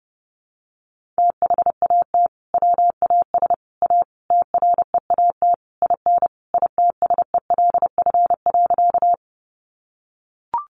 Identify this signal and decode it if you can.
Morse code, heard as a keyed tone, about 30 wpm, 700 Hz. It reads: THAT WAS A TREAT IN ITSELF.
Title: Sentences from Top 1300 Words 30wpm